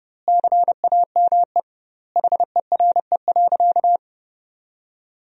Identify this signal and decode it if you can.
Morse code, heard as a keyed tone, about 30 wpm, 700 Hz. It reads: CAME HERE.
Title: 2x Word-Spacing - Sentences from Top 500 Words - 30wpm